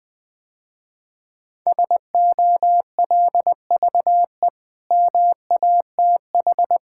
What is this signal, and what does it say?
Morse code, heard as a keyed tone, about 20 wpm, 700 Hz. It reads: SOLVE MATH
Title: Easy Sentences with Hints - 20wpm